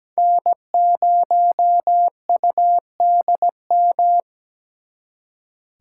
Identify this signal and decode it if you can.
Morse code, heard as a keyed tone, about 20 wpm, 700 Hz. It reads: N0UDM
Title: Speed Racing - USA 1x3 Call Signs 17wpm